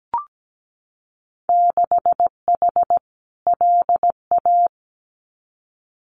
Morse code, heard as a keyed tone, about 17 wpm, 700 Hz. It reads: 6H LA
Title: Winter Field Day Exchanges Encore 17wpm